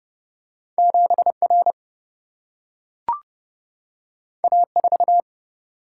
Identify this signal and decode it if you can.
Morse code, heard as a keyed tone, about 30 wpm, 700 Hz. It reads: 7 R A 4
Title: Single-Letter Single-Number 30wpm